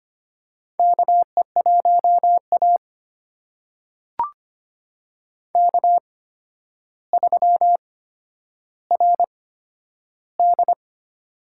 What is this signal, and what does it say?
Morse code, heard as a keyed tone, about 25 wpm, spaced as for 11 wpm, 700 Hz. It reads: KE1A K3RD
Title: USA Call Signs - 4 Characters - 8x Character Spacing - 25wpm